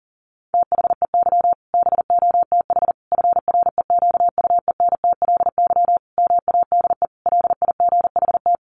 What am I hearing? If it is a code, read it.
Morse code, heard as a keyed tone, about 40 wpm, 700 Hz. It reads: THEY BOTH FREQUENTLY MADE LIGHT
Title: Sentences from Top 1300 Words 40wpm